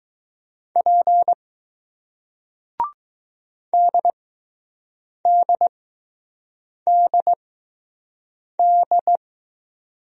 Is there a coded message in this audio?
Yes: Morse code from 0.8 to 9.2 s, heard as a keyed tone, about 20 wpm, spaced as for 6 wpm, 700 Hz. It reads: P DDDD